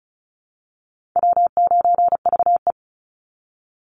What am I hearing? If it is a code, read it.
Morse code, heard as a keyed tone, about 34 wpm, 700 Hz. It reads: W9VE
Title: Call Signs Repeated 2x - 35wpm